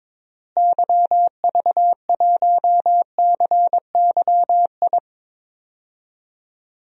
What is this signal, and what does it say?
Morse code, heard as a keyed tone, about 22 wpm, 700 Hz. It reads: YV1CYI
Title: DXCC-100 Call Signs - 6 Characters - Repeated 2x - 22wpm